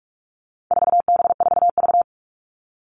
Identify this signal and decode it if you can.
Morse code, heard as a keyed tone, about 44 wpm, 700 Hz. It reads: 4B4V
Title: All Cut Numbers - 4-Digit 45wpm